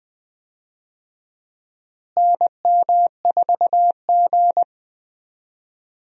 Morse code, heard as a keyed tone, about 21 wpm, 700 Hz. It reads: NM4G
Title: USA Call Signs - 4 Characters - Repeated 2x - 20wpm